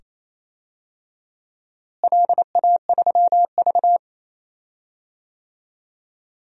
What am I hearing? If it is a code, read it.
Morse code, heard as a keyed tone, about 28 wpm, 700 Hz. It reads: LA3V
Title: DXCC-100 Call Signs - 4 Characters - Repeated 2x - 28wpm